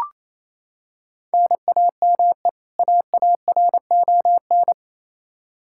Morse code, heard as a keyed tone, about 28 wpm, 700 Hz. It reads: NAME AARON
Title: Practice QSO - Exchange 1 - Beginning - 28wpm